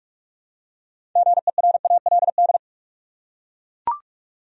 Morse code, heard as a keyed tone, about 41 wpm, 700 Hz. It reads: GERALD
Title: USA Common Names - 45wpm